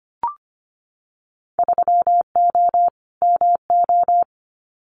Morse code, heard as a keyed tone, about 25 wpm, 700 Hz. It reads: 3O MO
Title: Winter Field Day Exchanges Encore 25wpm